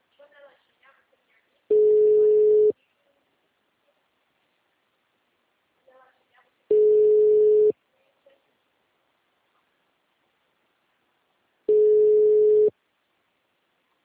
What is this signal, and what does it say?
Ringback tone (Europe, 425 Hz), with speech.